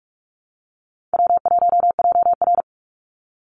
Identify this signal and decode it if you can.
Morse code, heard as a keyed tone, about 44 wpm, 700 Hz. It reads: W1JR
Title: Call Signs Repeated 2x - 45wpm